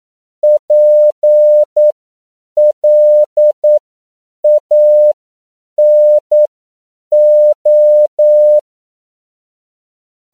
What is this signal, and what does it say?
Morse code, heard as a keyed tone, about 9 wpm, spaced as for 7 wpm, 600 Hz. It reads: PLANO
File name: CW / 10PPM